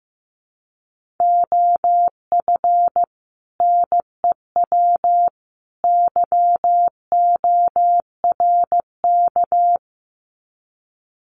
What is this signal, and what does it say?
Morse code, heard as a keyed tone, about 15 wpm, 700 Hz. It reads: OF NEW YORK
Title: Sets of 3 Words 15wpm